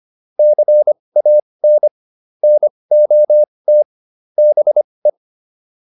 Morse code, heard as a keyed tone, about 25 wpm, spaced as for 20 wpm, 600 Hz. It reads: CAN NOT BE